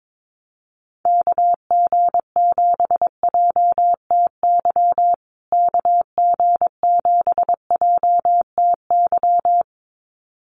Morse code, heard as a keyed tone, about 22 wpm, 700 Hz. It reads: KG7JTY KG7JTY
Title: USA Call Signs Repeated 2x 22wpm